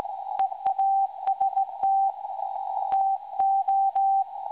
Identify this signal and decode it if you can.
Morse code, heard as a keyed tone, about 17 wpm, 780 Hz. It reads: EAST TO
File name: WebSDR records / SAQ recording 17.2kHz